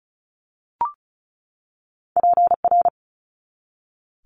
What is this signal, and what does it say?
Morse code, heard as a keyed tone, about 34 wpm, 700 Hz. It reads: PR
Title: Multi-letter Phonemes 35wpm